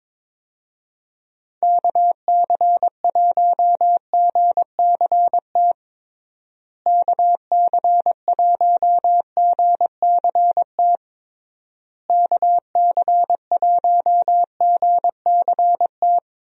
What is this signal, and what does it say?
Morse code, heard as a keyed tone, about 22 wpm, 700 Hz. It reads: KC1GCT KC1GCT KC1GCT
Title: USA Call Signs - 6 Characters - Repeated 3x - 22wpm